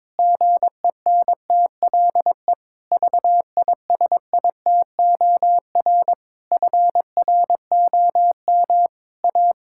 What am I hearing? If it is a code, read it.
Morse code, heard as a keyed tone, about 22 wpm, 700 Hz. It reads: GENTLE VISITOR FROM A
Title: Pangram Challenge v2 - 22wpm